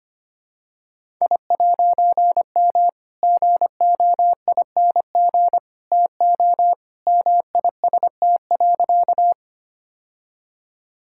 Morse code, heard as a keyed tone, about 25 wpm, 700 Hz. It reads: I'M GOING TO MIST.
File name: practice / 25wpm / All-Age Appropriate Jokes - 25wpm